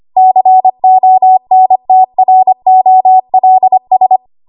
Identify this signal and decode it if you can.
Morse code, heard as a keyed tone, about 25 wpm, 750 Hz. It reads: CONTROLS
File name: Archive / 25 WPM / 25WPM